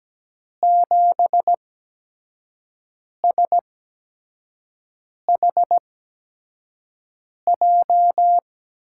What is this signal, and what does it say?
Morse code, heard as a keyed tone, about 17 wpm, spaced as for 5 wpm, 700 Hz. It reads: 7SHJ
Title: USA Call Signs - 6 Characters - 8x Character Spacing - 17wpm